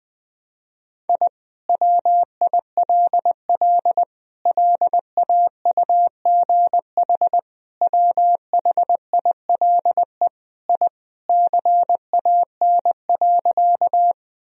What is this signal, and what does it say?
Morse code, heard as a keyed tone, about 20 wpm, 700 Hz. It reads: I WILL LAUGH WHILE I CAN.
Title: Sentences from Top 1000 Words - Review Part 1 - 20wpm